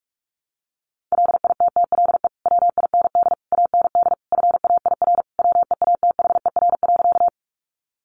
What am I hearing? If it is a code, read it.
Morse code, heard as a keyed tone, about 44 wpm, 700 Hz. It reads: LITTLE WIND AND FAIR WEATHER.
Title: Sentences from Top 1300 Words 45wpm